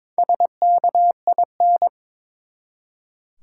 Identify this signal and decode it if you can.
Morse code, heard as a keyed tone, about 22 wpm, 700 Hz. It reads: SKIN
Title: Four Letter Words - 22wpm